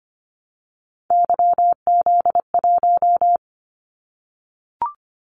Morse code, heard as a keyed tone, about 25 wpm, 700 Hz. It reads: YZ1
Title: Callsign Prefixes for Top 25 DXCC - 25wpm